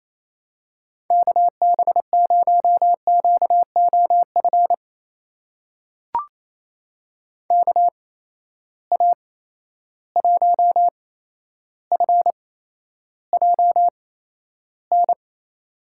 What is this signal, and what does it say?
Morse code, heard as a keyed tone, about 28 wpm, spaced as for 12 wpm, 700 Hz. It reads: KB0QOF KA1FJN
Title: USA Call Signs - 6 Characters - 8x Character Spacing - 28wpm